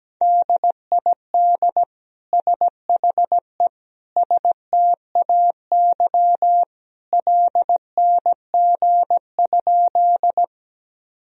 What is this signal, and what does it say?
Morse code, heard as a keyed tone, about 17 wpm, 700 Hz. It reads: DID SHE STAY LNG?
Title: Sentences from Top 500 Words and Abbreviations 17wpm